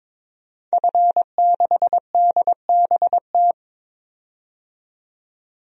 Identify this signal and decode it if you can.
Morse code, heard as a keyed tone, about 22 wpm, 700 Hz. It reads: F6DBT
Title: DXCC-100 Call Signs - 5 Characters - 22wpm